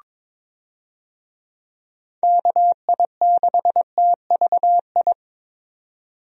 Morse code, heard as a keyed tone, about 22 wpm, 700 Hz. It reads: KI6TVI
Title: USA Call Signs - 6 Characters - 22wpm